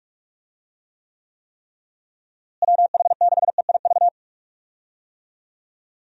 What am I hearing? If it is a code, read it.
Morse code, heard as a keyed tone, about 46 wpm, 700 Hz. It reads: WH6EIV
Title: USA Call Signs - 6 Characters - Repeated 3x - 45wpm